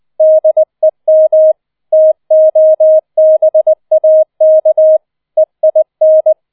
Morse code, heard as a keyed tone, about 20 wpm, 630 Hz. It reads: DEM TOBAK EIN